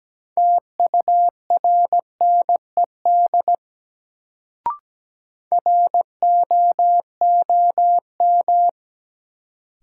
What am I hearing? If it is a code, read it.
Morse code, heard as a keyed tone, about 17 wpm, 700 Hz. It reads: TURNED ROOM
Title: Sets of 4 Words - New Word Drill - 17wpm